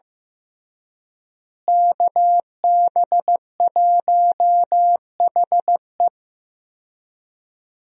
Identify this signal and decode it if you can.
Morse code, heard as a keyed tone, about 15 wpm, 700 Hz. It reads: KB1HE
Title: USA Call Signs - 5 Characters - Repeated 3x - 15wpm